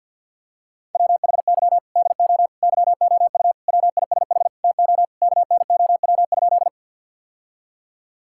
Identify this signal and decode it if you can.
Morse code, heard as a keyed tone, about 50 wpm, 700 Hz. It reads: WHY DO YOU WISH TO KNOW?
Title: Sentences from Top 1600 Words 50wpm